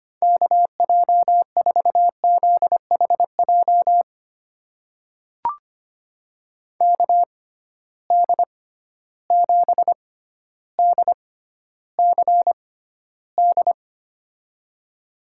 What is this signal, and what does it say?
Morse code, heard as a keyed tone, about 25 wpm, spaced as for 13 wpm, 700 Hz. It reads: KJ4ZHJ KD7DCD
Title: USA Call Signs - 6 Characters - 6x Character Spacing - 25wpm